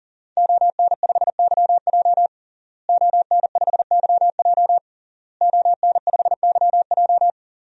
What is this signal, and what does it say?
Morse code, heard as a keyed tone, about 40 wpm, 700 Hz. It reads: ON5YJ ON5YJ ON5YJ
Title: DXCC-100 Call Signs - 5 Characters - Repeated 3x - 40wpm